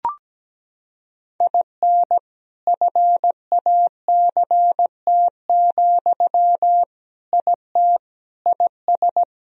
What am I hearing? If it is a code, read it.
Morse code, heard as a keyed tone, about 17 wpm, 700 Hz. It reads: IN FACT, IT IS
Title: Sentences from Top 1000 Words - Review Part 3 - 17wpm